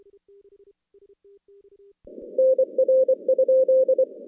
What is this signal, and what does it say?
Morse code, heard as a keyed tone, about 24 wpm, 520 Hz. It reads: NR?
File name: contests / CW-SS-k5tr- -2100